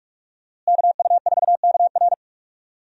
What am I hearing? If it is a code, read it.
Morse code, heard as a keyed tone, about 46 wpm, 700 Hz. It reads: KU4KR
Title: USA Call Signs - 5 Characters - Repeated 3x - 45wpm